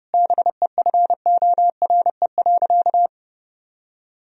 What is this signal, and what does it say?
Morse code, heard as a keyed tone, about 30 wpm, 700 Hz. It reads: BEFORE.
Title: Sentences from Top 1000 Words - Review Part 2 - 30wpm